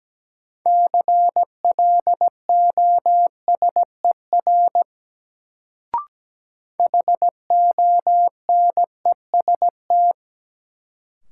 Morse code, heard as a keyed tone, about 17 wpm, 700 Hz. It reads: CLOSER HONEST
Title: Sets of 5 Words Encore - New Word Drill - 17wpm